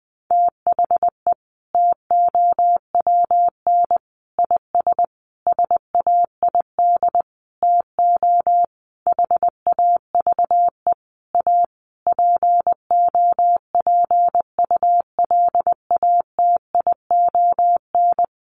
Morse code, heard as a keyed tone, about 20 wpm, 700 Hz. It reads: THE TOWN IS SAID TO HAVE A POPULATION
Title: Sentences from Top 1300 Words 20wpm